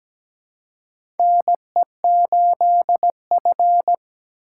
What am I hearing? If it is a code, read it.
Morse code, heard as a keyed tone, about 17 wpm, 700 Hz. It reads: NE8F
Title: USA Call Signs - 4 Characters - 4x Character Spacing - 17wpm